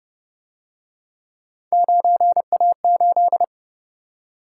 Morse code, heard as a keyed tone, about 29 wpm, 700 Hz. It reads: 9A8
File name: Speed-Racing - Three Letter-Number Combinations - 25wpm